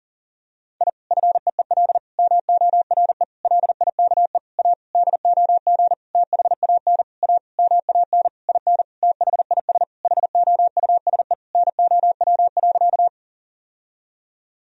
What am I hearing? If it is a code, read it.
Morse code, heard as a keyed tone, about 40 wpm, 700 Hz. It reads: I FEEL MORE LIKE A DOG THAN A MAN IN THIS HOUSE NOW.
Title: Sentences from Top 1000 Words - Review Part 2 - 40wpm